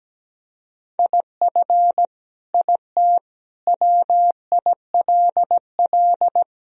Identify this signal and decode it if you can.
Morse code, heard as a keyed tone, about 17 wpm, 700 Hz. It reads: IF IT WILL
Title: Sentences from Top 1000 Words - Review Part 2 - 17wpm